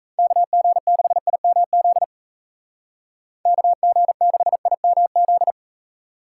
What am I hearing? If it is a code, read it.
Morse code, heard as a keyed tone, about 33 wpm, 700 Hz. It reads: KG6IMZ KG6IMZ
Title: Speed Racing - USA 2x3 Call Signs 28wpm